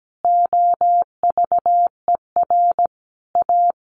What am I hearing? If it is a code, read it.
Morse code, heard as a keyed tone, about 17 wpm, 700 Hz. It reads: OVER A
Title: Sentences from Top 1300 Words 17wpm